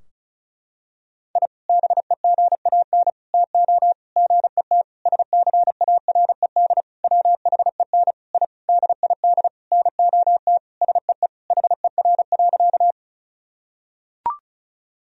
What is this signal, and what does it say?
Morse code, heard as a keyed tone, about 36 wpm, 700 Hz. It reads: I BEGAN TO GET SCARED WHEN I DID NOT SEE HER.
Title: Sentences from Top 1000 Words - Review Part 4 - 35wpm